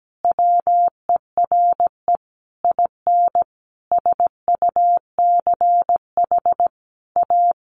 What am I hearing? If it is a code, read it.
Morse code, heard as a keyed tone, about 17 wpm, 700 Hz. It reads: WERE IN SUCH A
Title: Sentences from Top 1100 Words 17wpm